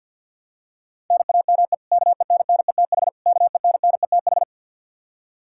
Morse code, heard as a keyed tone, about 47 wpm, 700 Hz. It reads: NAME KENNETH KENNETH
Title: Practice QSO - Exchange 1 - Beginning - 50wpm